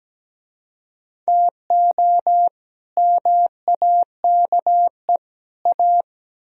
Morse code, heard as a keyed tone, about 20 wpm, 700 Hz. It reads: TO MAKE A
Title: Speed-Racing - Sets of 3 Words 17wpm